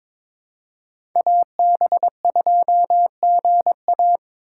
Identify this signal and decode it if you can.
Morse code, heard as a keyed tone, about 23 wpm, 700 Hz. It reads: AB2GA
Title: USA Call Signs - 5 Characters - 4x Character Spacing - 22wpm